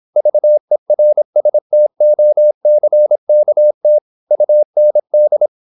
Morse code, heard as a keyed tone, about 26 wpm, 600 Hz. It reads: VERSTOCKT UND